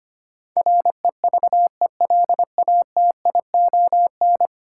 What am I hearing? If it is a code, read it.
Morse code, heard as a keyed tone, about 25 wpm, 700 Hz. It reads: REVELATION
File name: 10 Letter Words - First 3 Letters Given - 25wpm